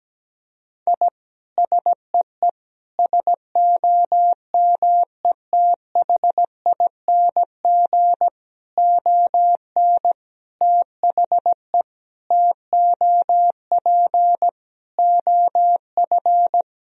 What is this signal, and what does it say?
Morse code, heard as a keyed tone, about 17 wpm, 700 Hz. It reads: I SEE SOMETHING ON THE TOP OF
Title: Sentences from Top 1000 Words - Review Part 2 - 17wpm